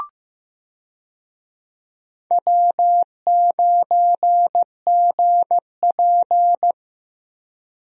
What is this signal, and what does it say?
Morse code, heard as a keyed tone, about 15 wpm, 700 Hz. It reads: W9GP
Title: USA Call Signs - 4 Characters - Repeated 2x - 15wpm